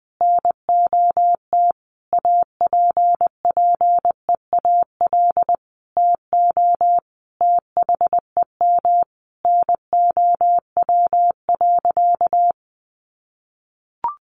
Morse code, heard as a keyed tone, about 19 wpm, 700 Hz. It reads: NOT APPEAL TO THEM NOW.
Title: Sentences from Top 1300 Words 20wpm